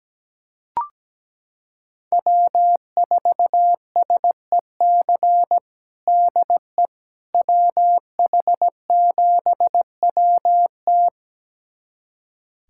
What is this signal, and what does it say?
Morse code, heard as a keyed tone, about 17 wpm, 700 Hz. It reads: W4SEC DE WH7WT
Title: USA Callsign DE Callsign - 5 Characters - 17wpm